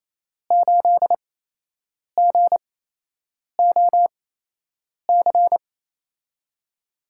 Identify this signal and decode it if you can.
Morse code, heard as a keyed tone, about 28 wpm, spaced as for 8 wpm, 700 Hz. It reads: 8GOC
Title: USA Call Signs - 5 Characters - 8x Character Spacing - 28wpm